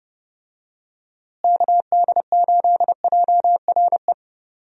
Morse code, heard as a keyed tone, about 32 wpm, 700 Hz. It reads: KD8JRE
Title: USA Call Signs - 6 Characters - Repeated 3x - 30wpm